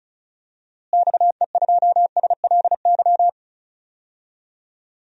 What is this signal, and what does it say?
Morse code, heard as a keyed tone, about 36 wpm, 700 Hz. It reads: XE2SLY
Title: DXCC-100 Call Signs - 6 Characters - Repeated 3x - 35wpm